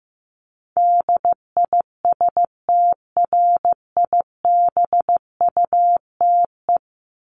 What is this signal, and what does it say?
Morse code, heard as a keyed tone, about 15 wpm, 700 Hz. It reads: DISTRIBUTE
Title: Top 1501-1750 Words 15wpm